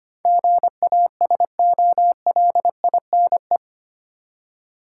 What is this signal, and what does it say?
Morse code, heard as a keyed tone, about 25 wpm, 700 Hz. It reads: GASOLINE
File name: practice / 25wpm / Easy Words with Hints - 25wpm